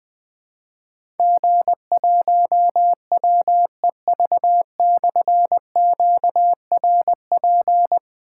Morse code, heard as a keyed tone, about 20 wpm, 700 Hz. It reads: G1WEV/QRP
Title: Difficult Call Signs 20wpm